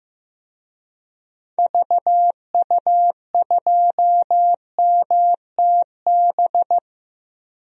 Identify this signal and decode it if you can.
Morse code, heard as a keyed tone, about 15 wpm, 700 Hz. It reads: VU2MTB
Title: DXCC-100 Call Signs - 6 Characters - Repeated 3x - 15wpm